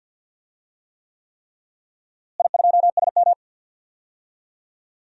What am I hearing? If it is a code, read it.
Morse code, heard as a keyed tone, about 52 wpm, 700 Hz. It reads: I2SM
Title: DXCC-100 Call Signs - 4 Characters - Repeated 3x - 50wpm